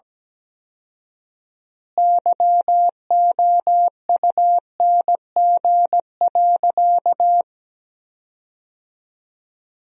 Morse code, heard as a keyed tone, about 17 wpm, 700 Hz. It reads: YOUNG.
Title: 4x Word-Spacing - Sentences from Top 400 Words - 17wpm